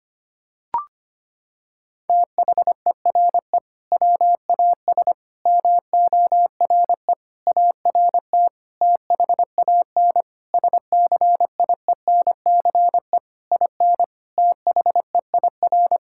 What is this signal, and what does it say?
Morse code, heard as a keyed tone, about 25 wpm, 700 Hz. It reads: THERE WAS MORE ART THAN SCIENCE IN THEIR
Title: Sentences from Top 1000 Words - Review Part 6 - 25wpm